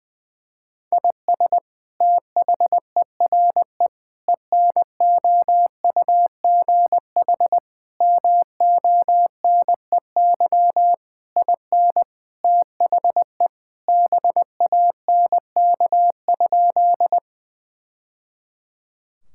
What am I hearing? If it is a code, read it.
Morse code, heard as a keyed tone, about 20 wpm, 700 Hz. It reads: IS THERE ENOUGH MONEY IN THE BANK?
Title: Sentences from Top 1000 Words - Review Part 6 - 20wpm